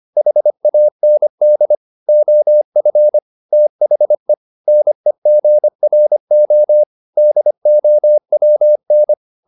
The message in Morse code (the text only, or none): HAND OF THE NEGRO DOWN